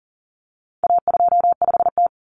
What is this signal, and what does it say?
Morse code, heard as a keyed tone, about 40 wpm, 700 Hz. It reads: A25T
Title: Common Cut Numbers - 4-Digit 40wpm